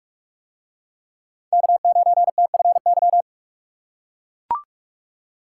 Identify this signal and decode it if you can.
Morse code, heard as a keyed tone, about 44 wpm, 700 Hz. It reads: K9TFY